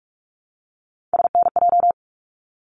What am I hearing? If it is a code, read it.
Morse code, heard as a keyed tone, about 44 wpm, 700 Hz. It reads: SNJ
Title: ARRL Sections 45wpm